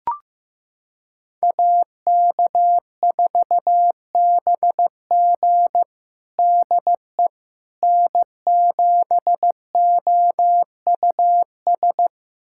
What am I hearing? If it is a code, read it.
Morse code, heard as a keyed tone, about 15 wpm, 700 Hz. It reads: AK4BG DE N7OUS
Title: USA Callsign DE Callsign - 5 Characters - 15wpm